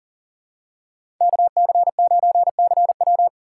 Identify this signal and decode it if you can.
Morse code, heard as a keyed tone, about 40 wpm, 700 Hz. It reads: KC9CW
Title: USA Call Signs - 2x Character Spacing - 40wpm